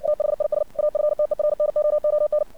Morse code, heard as a keyed tone, about 59 wpm, 600 Hz. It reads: THIS AFTERNOON
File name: archived / CRSnet